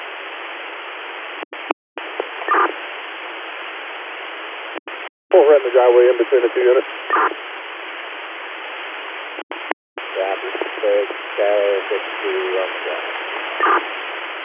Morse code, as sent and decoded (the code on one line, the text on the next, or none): none